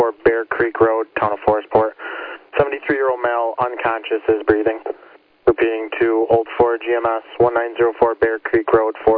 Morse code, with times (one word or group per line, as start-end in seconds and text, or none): none